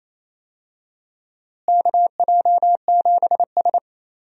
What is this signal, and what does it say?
Morse code, heard as a keyed tone, about 28 wpm, 700 Hz. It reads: KJ7S
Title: USA Call Signs - 4 Characters - Repeated 2x - 28wpm